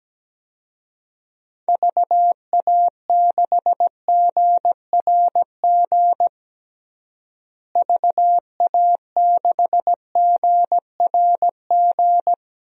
Morse code, heard as a keyed tone, about 17 wpm, 700 Hz. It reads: VA6GRG VA6GRG
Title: DXCC-100 Call Signs - 6 Characters - Repeated 3x - 17wpm